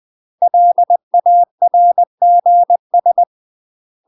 Morse code, heard as a keyed tone, about 20 wpm, 700 Hz. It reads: LARGS
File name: UK MASTER QSOs TRAINING/20WPM